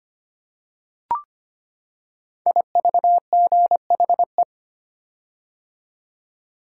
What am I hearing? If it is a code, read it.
Morse code, heard as a keyed tone, about 25 wpm, 700 Hz. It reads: IVGHE
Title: Code Groups - 5 Letters - 25wpm